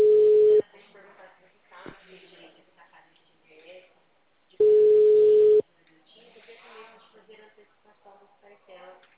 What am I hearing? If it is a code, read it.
Ringback tone (Europe, 425 Hz), with speech.